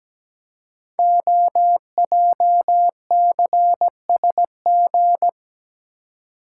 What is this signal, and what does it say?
Morse code, heard as a keyed tone, about 17 wpm, 700 Hz. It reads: OJCSG
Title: Code Groups - 5 Letters - 17wpm